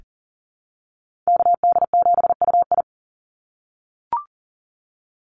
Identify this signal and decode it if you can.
Morse code, heard as a keyed tone, about 38 wpm, 700 Hz. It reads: KD7UI
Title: Call Signs - 40wpm